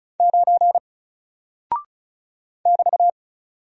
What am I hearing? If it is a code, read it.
Morse code, heard as a keyed tone, about 36 wpm, 700 Hz. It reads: 9 <BT>
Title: Single Character (Letter:Number:Punctuation:Prosign) Alt 35wpm